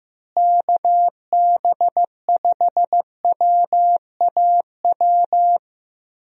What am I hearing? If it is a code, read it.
Morse code, heard as a keyed tone, about 15 wpm, 700 Hz. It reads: KB5WAW
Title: USA Call Signs - 6 Characters - Repeated 3x - 15wpm